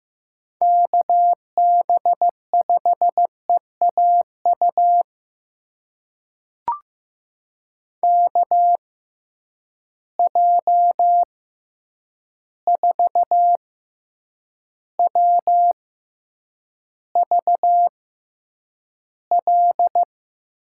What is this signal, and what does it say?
Morse code, heard as a keyed tone, about 15 wpm, spaced as for 8 wpm, 700 Hz. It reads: KB5EAU KJ4WVL
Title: USA Call Signs - 6 Characters - 6x Character Spacing - 15wpm